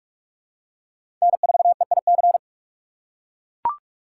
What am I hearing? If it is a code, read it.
Morse code, heard as a keyed tone, about 42 wpm, 700 Hz. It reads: N4EIC
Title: USA Call Signs - 5 Characters - 2x Character Spacing - 45wpm